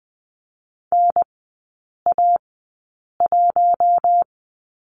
Morse code, heard as a keyed tone, about 20 wpm, spaced as for 8 wpm, 700 Hz. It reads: NA1
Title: 041 Lesson 8e - Number 1 - Three Characters 20wpm